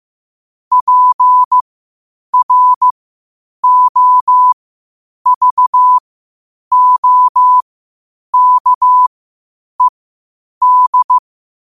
Morse code, heard as a keyed tone, about 15 wpm, spaced as for 8 wpm, 1000 Hz. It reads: PROVOKED
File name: qotdfiles / 8wpm